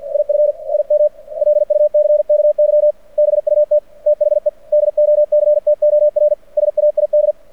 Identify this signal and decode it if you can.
Morse code, heard as a keyed tone, about 48 wpm, 600 Hz. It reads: IF IM LUCKY BUT THE DOCTOR SAID
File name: archived / CRSnet